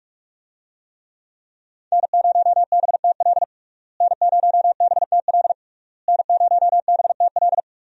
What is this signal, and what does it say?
Morse code, heard as a keyed tone, about 46 wpm, 700 Hz. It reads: N0BTL N0BTL N0BTL
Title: USA Call Signs - 5 Characters - Repeated 3x - 45wpm